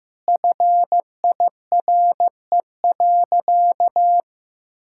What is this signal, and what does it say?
Morse code, heard as a keyed tone, about 15 wpm, 700 Hz. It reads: FIRE.
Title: Sentences from Top 1600 Words 15wpm